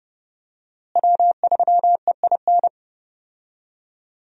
Morse code, heard as a keyed tone, about 32 wpm, 700 Hz. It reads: W3EIN
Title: USA Call Signs - 5 Characters - Repeated 3x - 30wpm